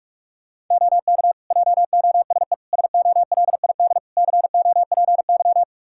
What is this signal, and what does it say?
Morse code, heard as a keyed tone, about 46 wpm, 700 Hz. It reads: OK JOSE SOLID COPY
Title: Practice QSO - Exchange 3 - More Info - 45wpm